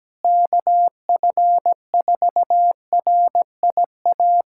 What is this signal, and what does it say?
Morse code, heard as a keyed tone, about 17 wpm, 700 Hz. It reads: KF4RIA
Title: USA Callsign DE Callsign 17wpm